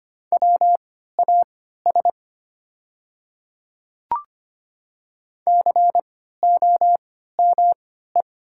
Morse code, heard as a keyed tone, about 25 wpm, spaced as for 14 wpm, 700 Hz. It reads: WAS COME
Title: 3x Character Spacing - Top 100 Words - 25wpm